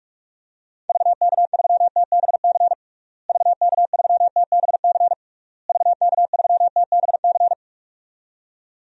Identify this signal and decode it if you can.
Morse code, heard as a keyed tone, about 42 wpm, 700 Hz. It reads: VK3TBC VK3TBC VK3TBC
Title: DXCC-100 Call Signs - 6 Characters - Repeated 3x - 45wpm